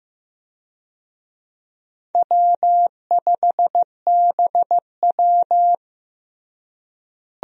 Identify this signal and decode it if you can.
Morse code, heard as a keyed tone, about 15 wpm, 700 Hz. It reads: W5BW
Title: USA Call Signs - 4 Characters - Repeated 2x - 15wpm